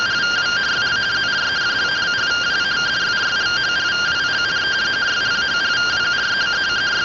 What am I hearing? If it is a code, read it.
Two-tone teleprinter signal, about 170 Hz shift.